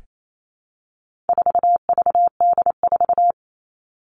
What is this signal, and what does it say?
Morse code, heard as a keyed tone, about 28 wpm, 700 Hz. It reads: 4VD4
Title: All Cut Numbers - 4-Digit 28wpm